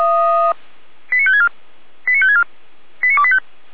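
Five-tone selective call (ZVEI), heard as quick runs of five tones, tones of about 70 ms. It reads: 89564 89564 89285
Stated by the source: ZVEI five-tone call 89025 received by SDR.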